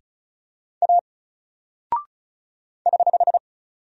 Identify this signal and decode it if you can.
Morse code, heard as a keyed tone, about 35 wpm, 700 Hz. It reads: A <HH>
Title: Single Character (Letter:Number:Punctuation:Prosign) 35wpm